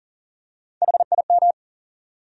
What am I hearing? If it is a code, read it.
Morse code, heard as a keyed tone, about 40 wpm, 700 Hz. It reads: HIM
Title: Top 100 Words 40wpm